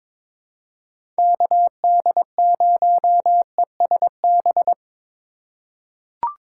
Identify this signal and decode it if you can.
Morse code, heard as a keyed tone, about 22 wpm, 700 Hz. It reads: KD0ESB
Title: USA Call Signs - 6 Characters - 3x Character Spacing - 22wpm